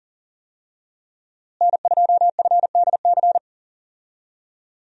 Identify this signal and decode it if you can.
Morse code, heard as a keyed tone, about 40 wpm, 700 Hz. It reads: N2FDC
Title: USA Call Signs - 5 Characters - Repeated 2x - 40wpm